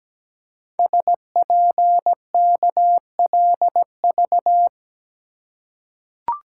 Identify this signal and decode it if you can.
Morse code, heard as a keyed tone, about 17 wpm, 700 Hz. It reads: SPKLV
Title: Code Groups - 5 Letters - 17wpm